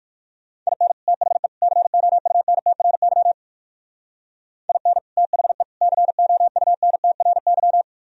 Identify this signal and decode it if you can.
Morse code, heard as a keyed tone, about 55 wpm, 700 Hz. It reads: IN THE COUNTRY IN THE COUNTRY
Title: Speed-Racing - Sets of 3 Words 45wpm